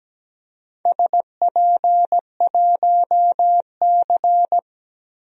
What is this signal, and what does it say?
Morse code, heard as a keyed tone, about 17 wpm, 700 Hz. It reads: SP1C
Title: DXCC-100 Call Signs - 4 Characters - 17wpm